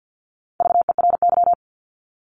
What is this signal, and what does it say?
Morse code, heard as a keyed tone, about 50 wpm, 700 Hz. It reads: VERY
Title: Top 200 Words 50wpm